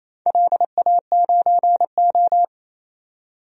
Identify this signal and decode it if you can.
Morse code, heard as a keyed tone, about 28 wpm, 700 Hz. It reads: LA9O
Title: DXCC-100 Call Signs - 4 Characters - 28wpm